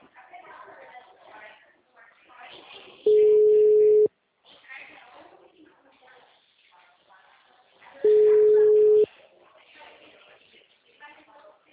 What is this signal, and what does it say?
Ringback tone (Europe, 425 Hz), with speech.